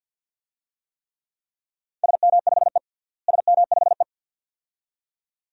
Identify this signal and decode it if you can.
Morse code, heard as a keyed tone, about 52 wpm, 700 Hz. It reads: SM5E SM5E
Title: DXCC-100 Call Signs - 4 Characters - Repeated 2x - 50wpm